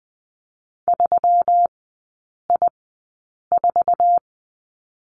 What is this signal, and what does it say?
Morse code, heard as a keyed tone, about 20 wpm, spaced as for 8 wpm, 700 Hz. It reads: 3I4
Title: 127 Lesson 21e - Number 6 - Three Characters 20wpm